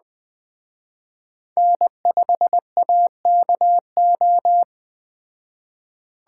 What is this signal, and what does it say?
Morse code, heard as a keyed tone, about 20 wpm, 700 Hz. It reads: N5AKO
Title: Speed Racing - USA 1x3 Call Signs 17wpm